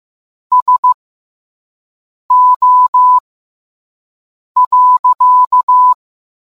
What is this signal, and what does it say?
Morse code, heard as a keyed tone, about 15 wpm, spaced as for 5 wpm, 1000 Hz. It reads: SO.